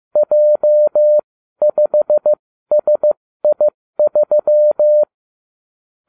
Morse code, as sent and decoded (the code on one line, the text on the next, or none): .--- ..... ... .. ...--
J5SI3